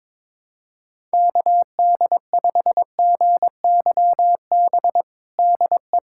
Morse code, heard as a keyed tone, about 22 wpm, 700 Hz. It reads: KD5GYB DE
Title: USA Callsign DE Callsign - 6 Characters - 22wpm